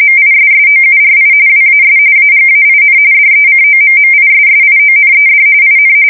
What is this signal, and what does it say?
RTTY, about 45.45 baud. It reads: MISSOURI WAS AGAIN DECOMMISSIONED IN
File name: fsk170 / USS-Missouri